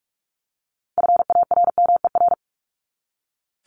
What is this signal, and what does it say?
Morse code, heard as a keyed tone, about 44 wpm, 700 Hz. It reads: FARMER
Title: Top 1751-2000 Words 45wpm